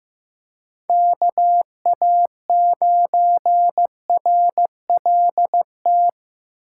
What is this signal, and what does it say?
Morse code, heard as a keyed tone, about 15 wpm, 700 Hz. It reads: KA9RLT
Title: USA Call Signs - 8x Character Spacing - 15wpm